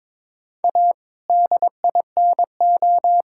Morse code, heard as a keyed tone, about 22 wpm, 700 Hz. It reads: A DINO
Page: morse-code-ninja-practice-22wpm